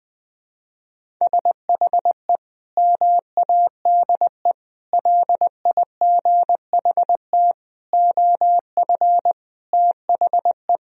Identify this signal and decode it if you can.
Morse code, heard as a keyed tone, about 21 wpm, 700 Hz. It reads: SHE MADE LIGHT OF THE
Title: Sentences from Top 1000 Words - Review Part 1 - 20wpm